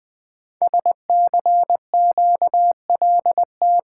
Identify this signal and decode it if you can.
Morse code, heard as a keyed tone, about 20 wpm, 700 Hz. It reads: SCQLT
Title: Code Groups - 5 Letters - 20wpm